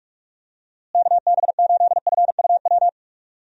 Morse code, heard as a keyed tone, about 46 wpm, 700 Hz. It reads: KB8FUW
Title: USA Call Signs - 6 Characters - Repeated 3x - 45wpm